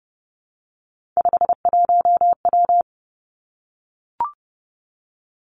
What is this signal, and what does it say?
Morse code, heard as a keyed tone, about 30 wpm, 700 Hz. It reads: 51W
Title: Three Challenging Letter-Number Combinations 30wpm